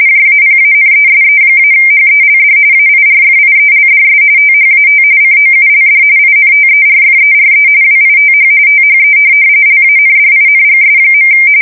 RTTY, about 45.45 baud. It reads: JAPANESE-HELD AIRFIELDS THAT COULD INTERFERE WITH THE LANDINGS.
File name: fsk170 / USS-Massachusetts